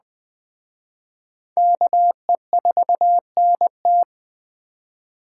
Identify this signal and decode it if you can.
Morse code, heard as a keyed tone, about 20 wpm, 700 Hz. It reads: KE4NT
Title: Speed Racing - USA 2x2 Call Signs 17wpm